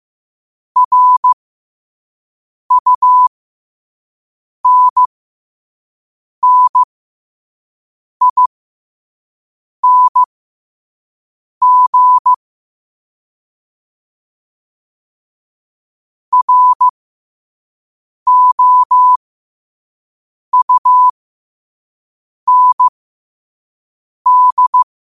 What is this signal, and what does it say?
Morse code, heard as a keyed tone, about 15 wpm, spaced as for 5 wpm, 1000 Hz. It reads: RUNNING ROUND
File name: qotdfiles / 5wpm